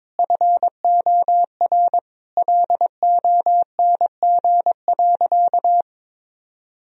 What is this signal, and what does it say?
Morse code, heard as a keyed tone, about 22 wpm, 700 Hz. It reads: FOR LONG.
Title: Sentences from Top 1000 Words - Review Part 2 - 22wpm